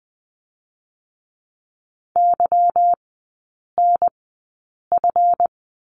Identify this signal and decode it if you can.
Morse code, heard as a keyed tone, about 20 wpm, spaced as for 8 wpm, 700 Hz. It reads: YNF
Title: 152 Lesson 25e - Letter p - Three Characters 20wpm